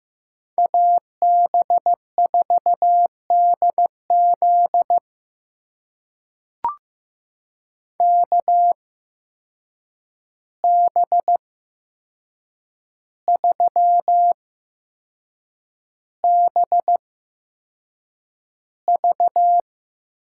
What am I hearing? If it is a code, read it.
Morse code, heard as a keyed tone, about 15 wpm, spaced as for 6 wpm, 700 Hz. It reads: AB4DZ KB3BV
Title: USA Call Signs - 5 Characters - 8x Character Spacing - 15wpm